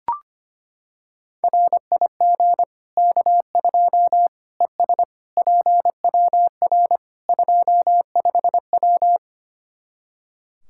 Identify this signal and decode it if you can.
Morse code, heard as a keyed tone, about 25 wpm, 700 Hz. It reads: RIG K2 ES PWR 25W
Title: Practice QSO - Exchange 2 - Info - 25wpm